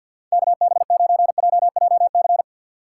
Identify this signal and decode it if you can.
Morse code, heard as a keyed tone, about 52 wpm, 700 Hz. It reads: KB9JJC
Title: USA Call Signs - 6 Characters - 4x Character Spacing - 50wpm